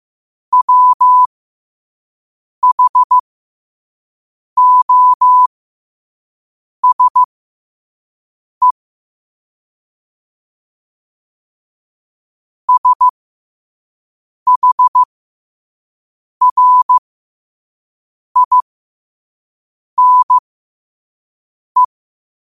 Morse code, heard as a keyed tone, about 15 wpm, spaced as for 5 wpm, 1000 Hz. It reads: WHOSE SHRINE
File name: qotdfiles / 5wpm